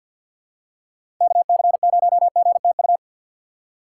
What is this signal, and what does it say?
Morse code, heard as a keyed tone, about 53 wpm, 700 Hz. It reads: KC0GTU
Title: USA Call Signs - 6 Characters - 3x Character Spacing - 50wpm